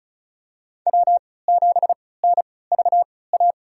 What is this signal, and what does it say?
Morse code, heard as a keyed tone, about 36 wpm, spaced as for 20 wpm, 700 Hz. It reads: W7NVA
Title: USA Call Signs - 5 Characters - 3x Character Spacing - 35wpm